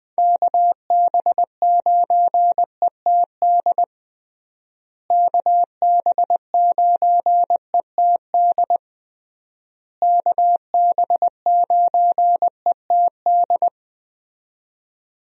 Morse code, heard as a keyed tone, about 20 wpm, 700 Hz. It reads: KB9ETD KB9ETD KB9ETD
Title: USA Call Signs - 6 Characters - Repeated 3x - 20wpm